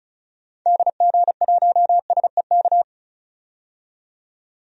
Morse code, heard as a keyed tone, about 36 wpm, 700 Hz. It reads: DG1SEK
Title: DXCC-100 Call Signs - 6 Characters - 35wpm